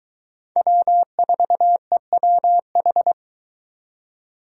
Morse code, heard as a keyed tone, about 22 wpm, 700 Hz. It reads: W4EWH